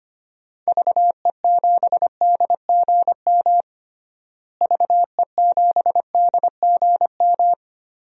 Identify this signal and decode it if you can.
Morse code, heard as a keyed tone, about 25 wpm, 700 Hz. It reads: VE7DGM VE7DGM
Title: DXCC-100 Call Signs - 6 Characters - Repeated 2x - 25wpm